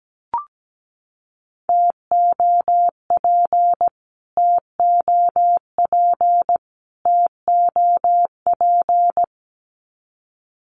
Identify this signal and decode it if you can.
Morse code, heard as a keyed tone, about 17 wpm, 700 Hz. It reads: TOP TOP TOP
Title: Top 500 Words Repeated 17wpm